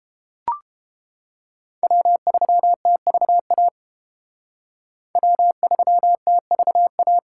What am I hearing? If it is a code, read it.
Morse code, heard as a keyed tone, about 29 wpm, 700 Hz. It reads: W3TVA W3TVA